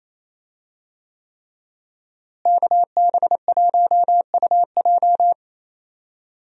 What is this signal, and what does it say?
Morse code, heard as a keyed tone, about 28 wpm, 700 Hz. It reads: KB1UJ
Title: USA Call Signs - 5 Characters - Repeated 3x - 28wpm